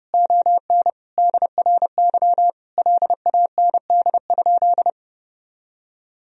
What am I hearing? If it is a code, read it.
Morse code, heard as a keyed tone, about 32 wpm, 700 Hz. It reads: ON DRY LAND?
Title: Sentences from Top 1000 Words - Review Part 3 - 30wpm